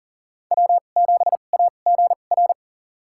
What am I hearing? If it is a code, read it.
Morse code, heard as a keyed tone, about 40 wpm, spaced as for 29 wpm, 700 Hz. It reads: W7AGR
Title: USA Call Signs - 5 Characters - 2x Character Spacing - 40wpm